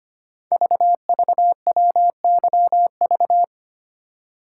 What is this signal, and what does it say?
Morse code, heard as a keyed tone, about 25 wpm, 700 Hz. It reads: VVWYV
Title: Code Groups - 5 Letters - 25wpm